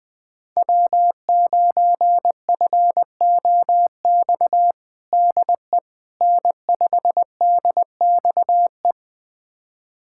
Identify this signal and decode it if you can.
Morse code, heard as a keyed tone, about 21 wpm, 700 Hz. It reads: W9FOX DE N5DXE
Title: USA Callsign DE Callsign - 5 Characters - 20wpm